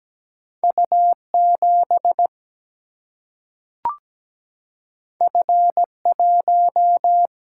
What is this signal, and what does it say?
Morse code, heard as a keyed tone, about 17 wpm, 700 Hz. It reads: U7 F1
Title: Single-Letter Single-Number 17wpm